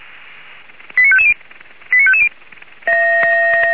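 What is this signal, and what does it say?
Five-tone selective call (ZVEI), heard as quick runs of five tones, tones of about 70 ms. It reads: 89550 89550
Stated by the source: Five-tone ZVEI selective call received by SDR, archived by Feuerwehr Orlamünde (fire brigade).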